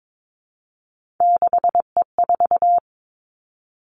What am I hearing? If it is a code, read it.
Morse code, heard as a keyed tone, about 22 wpm, 700 Hz. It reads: 6E4
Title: Three Challenging Letter-Number Combinations 22wpm